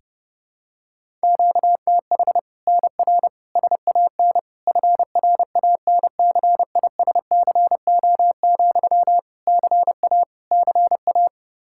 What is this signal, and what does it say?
Morse code, heard as a keyed tone, about 32 wpm, 700 Hz. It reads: QTH NR SAN FRANCISCO, CA CA
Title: Practice QSO - Exchange 1 - Beginning - 30wpm